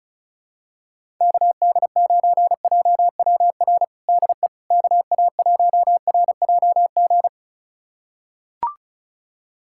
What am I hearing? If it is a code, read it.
Morse code, heard as a keyed tone, about 36 wpm, 700 Hz. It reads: KD9JWR DE KA1RJG
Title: USA Callsign DE Callsign 35wpm